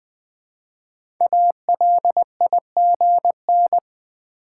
Morse code, heard as a keyed tone, about 20 wpm, 700 Hz. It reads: ALIGN
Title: 5 Letter Words - First 3 Letters Given - 20wpm